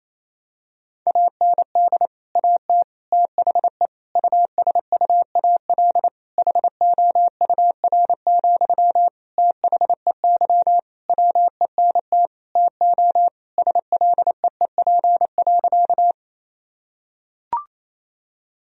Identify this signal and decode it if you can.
Morse code, heard as a keyed tone, about 27 wpm, 700 Hz. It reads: AND AT THE USUAL HOUR, THEY WENT TO SLEEP.
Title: Sentences from Top 1000 Words - Review Part 2 - 28wpm